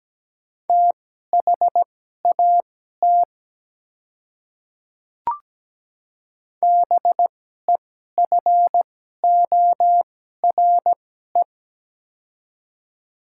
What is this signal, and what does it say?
Morse code, heard as a keyed tone, about 17 wpm, spaced as for 12 wpm, 700 Hz. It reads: THAT BEFORE
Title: 2x Character Spacing - Top 200 Words - 17wpm